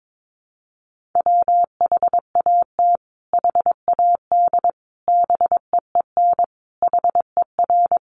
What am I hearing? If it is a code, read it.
Morse code, heard as a keyed tone, about 22 wpm, 700 Hz. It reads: WHAT HAD BEEN HER
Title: Sentences from Top 1500 Words 22wpm